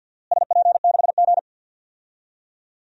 Morse code, heard as a keyed tone, about 52 wpm, 700 Hz. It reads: SP6G
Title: DXCC-100 Call Signs - 4 Characters - Repeated 2x - 50wpm